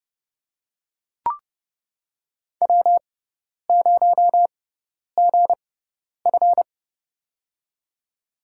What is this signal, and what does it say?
Morse code, heard as a keyed tone, about 30 wpm, spaced as for 10 wpm, 700 Hz. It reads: W0GF
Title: USA Call Signs - 4 Characters - 6x Character Spacing - 30wpm